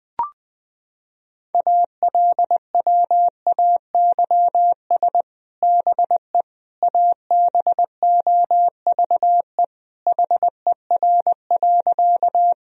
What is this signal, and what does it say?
Morse code, heard as a keyed tone, about 20 wpm, 700 Hz. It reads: ALWAYS BE ABOVE HER.
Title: Sentences from Top 1000 Words - Review Part 2 - 20wpm